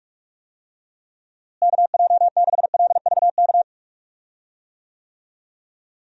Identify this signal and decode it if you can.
Morse code, heard as a keyed tone, about 47 wpm, 700 Hz. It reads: KJ6LVK
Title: USA Call Signs - 6 Characters - Repeated 3x - 45wpm